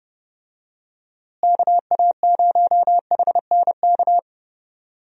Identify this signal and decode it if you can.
Morse code, heard as a keyed tone, about 30 wpm, 700 Hz. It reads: KA0HNK
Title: USA Call Signs - 6 Characters - Repeated 3x - 30wpm